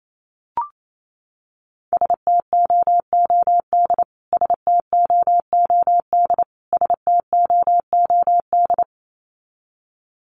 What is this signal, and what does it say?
Morse code, heard as a keyed tone, about 28 wpm, 700 Hz. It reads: STOOD STOOD STOOD
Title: Top 500 Words Repeated 28wpm